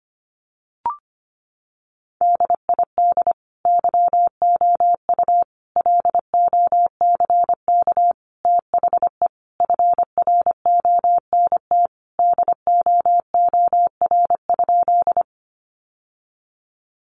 Morse code, heard as a keyed tone, about 25 wpm, 700 Hz. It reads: DID YOU LOCK THE FRONT DOOR?
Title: Sentences from Top 1300 Words 25wpm